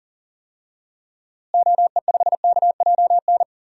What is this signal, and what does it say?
Morse code, heard as a keyed tone, about 40 wpm, 700 Hz. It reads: OE5KJN
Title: DXCC-100 Call Signs - 6 Characters - Repeated 3x - 40wpm